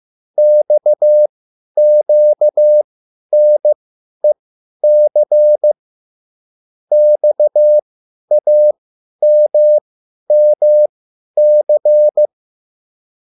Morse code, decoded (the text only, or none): XQNEC XAMMC